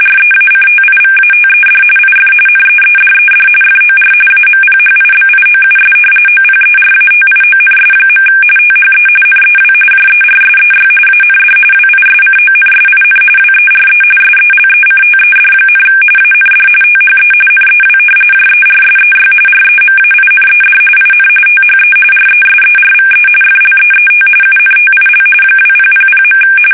RTTY, about 45.45 baud. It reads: REACTIVATED IF NEEDED FOR AMPHIBIOUS OPERATIONS. THE NEW JERSEY AND WISCONSIN WERE SELECTED, AND THE NEW JERSEY REMAINED IN THE RESERVE FLEET UNTIL 2000.
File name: fsk850 / battleshipnj